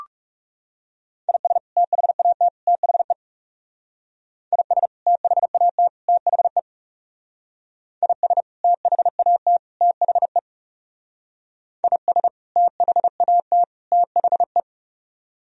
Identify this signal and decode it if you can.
Morse code, heard as a keyed tone, about 36 wpm, 700 Hz. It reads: IS THAT THE IS THAT THE IS THAT THE IS THAT THE
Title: Speed-Racing - Sets of 3 Words 30wpm